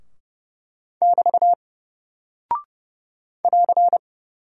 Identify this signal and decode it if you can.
Morse code, heard as a keyed tone, about 30 wpm, 700 Hz. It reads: <BT> <AR>
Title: Single Character (Letter:Number:Punctuation:Prosign) Alt 30wpm